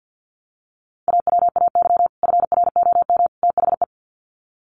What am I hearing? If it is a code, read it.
Morse code, heard as a keyed tone, about 48 wpm, 700 Hz. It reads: AWAY FROM THE
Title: Sets of 3 Words 50wpm